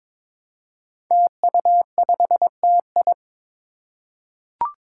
Morse code, heard as a keyed tone, about 21 wpm, 700 Hz. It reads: TU5TI
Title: Five Challenging Letter-Number Combinations 22wpm